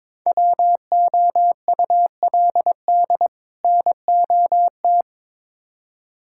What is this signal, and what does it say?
Morse code, heard as a keyed tone, about 23 wpm, 700 Hz. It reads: WOULD NOT
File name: practice / 22wpm / Build A Phrase - 5-Word Phrases - 22wpm